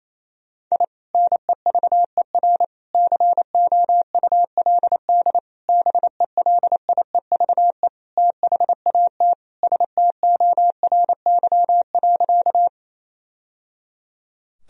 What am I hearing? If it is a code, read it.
Morse code, heard as a keyed tone, about 28 wpm, 700 Hz. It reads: I NEVER COULD BELIEVE THAT STORY.
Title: Sentences from Top 1000 Words - Review Part 1 - 28wpm